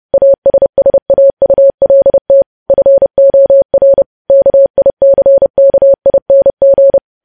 Morse code, heard as a keyed tone, about 30 wpm, 550 Hz. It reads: ASSAULT FOR KICKING
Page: news headlines in morse code at 30 wpm